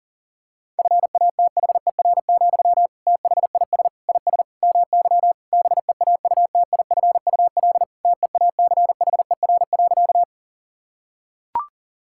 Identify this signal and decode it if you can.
Morse code, heard as a keyed tone, about 37 wpm, 700 Hz. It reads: FATHER, THIS IS MY BEAUTIFUL TEACHER.
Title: Sentences from Top 1000 Words - Review Part 6 - 40wpm